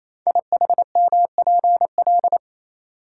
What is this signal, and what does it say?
Morse code, heard as a keyed tone, about 27 wpm, 700 Hz. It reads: IHMPL
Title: Code Groups - 5 Letters - 28wpm